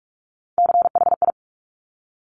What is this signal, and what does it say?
Morse code, heard as a keyed tone, about 46 wpm, 690 Hz. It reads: CHI
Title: Top Three Letter Combinations in Words 45wpm